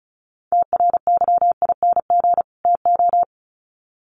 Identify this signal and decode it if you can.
Morse code, heard as a keyed tone, about 34 wpm, 700 Hz. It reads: TRYING TO
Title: Sets of 2 Words 35wpm